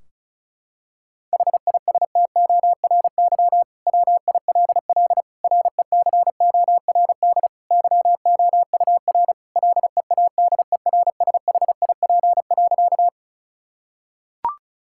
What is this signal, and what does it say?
Morse code, heard as a keyed tone, about 36 wpm, 700 Hz. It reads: HISTORY WILL RECORD YOUR LEADERSHIP.
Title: Sentences from Top 1600 Words 35wpm